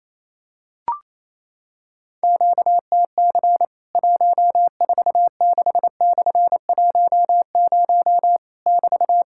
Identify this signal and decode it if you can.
Morse code, heard as a keyed tone, about 28 wpm, 700 Hz. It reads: QTC 146/10 <BT>
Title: Worked All Europe DX Contest - QTC Exchange - 28wpm